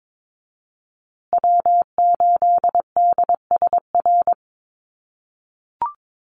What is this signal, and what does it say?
Morse code, heard as a keyed tone, about 22 wpm, 700 Hz. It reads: W8DSR
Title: USA Call Signs Repeated 3x 22wpm